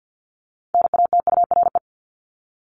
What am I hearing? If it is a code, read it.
Morse code, heard as a keyed tone, about 50 wpm, 700 Hz. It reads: NATURE
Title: Top 501-750 Words 50wpm